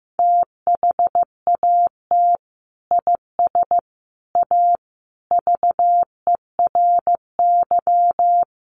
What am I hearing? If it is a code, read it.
Morse code, heard as a keyed tone, about 15 wpm, 700 Hz. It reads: THAT IS A VERY
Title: Sentences from Top 1200 Words 15wpm